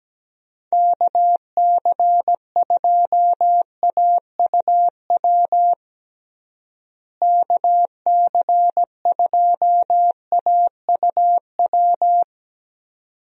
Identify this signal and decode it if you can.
Morse code, heard as a keyed tone, about 17 wpm, 700 Hz. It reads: KC2AUW KC2AUW
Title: USA Call Signs - 6 Characters - Repeated 2x - 17wpm